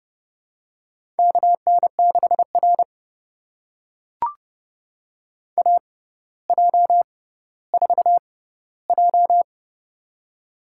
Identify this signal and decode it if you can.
Morse code, heard as a keyed tone, about 30 wpm, spaced as for 15 wpm, 700 Hz. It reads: KN6R AJ4J
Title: USA Call Signs - 4 Characters - 6x Character Spacing - 30wpm